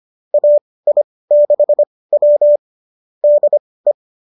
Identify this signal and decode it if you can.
Morse code, heard as a keyed tone, about 25 wpm, spaced as for 18 wpm, 590 Hz. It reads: AI6W DE